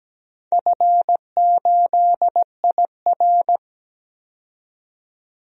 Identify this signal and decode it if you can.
Morse code, heard as a keyed tone, about 17 wpm, 700 Hz. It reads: F8IR
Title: DXCC-100 Call Signs - 4 Characters - 17wpm